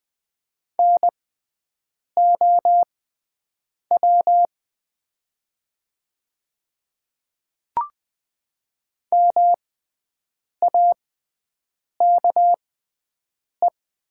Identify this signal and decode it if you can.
Morse code, heard as a keyed tone, about 20 wpm, spaced as for 7 wpm, 700 Hz. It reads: NOW MAKE
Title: 6x Character Spacing - Top 200 Words - 20wpm